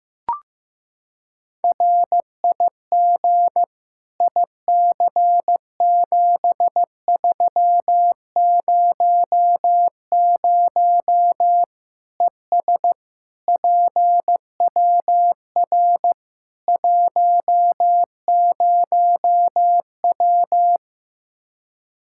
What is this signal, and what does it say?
Morse code, heard as a keyed tone, about 16 wpm, 700 Hz. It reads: RIG IC7300 ES PWR 10W
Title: K1USN SST QSO Practice 15wpm